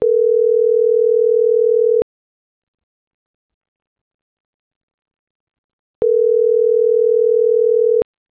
Ringback tone (North America), 440 + 480 Hz.